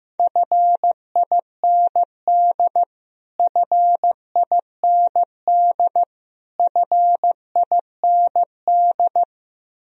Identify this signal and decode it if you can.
Morse code, heard as a keyed tone, about 15 wpm, 700 Hz. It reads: FIND FIND FIND
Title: Top 100 Words Repeated 15wpm